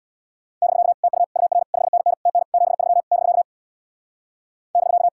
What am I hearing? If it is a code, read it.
Morse code, heard as a keyed tone, about 80 wpm, 700 Hz. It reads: <BK> TU UR 5NN TN 73 <BK> <BK>